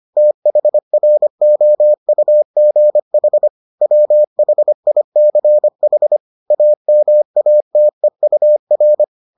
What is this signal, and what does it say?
Morse code, heard as a keyed tone, about 25 wpm, 600 Hz. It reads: THROUGH WHICH AMATEUR